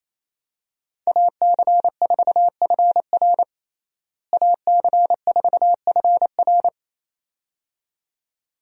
Morse code, heard as a keyed tone, about 28 wpm, 700 Hz. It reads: AC4FR AC4FR
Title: USA Call Signs - 5 Characters - Repeated 2x - 28wpm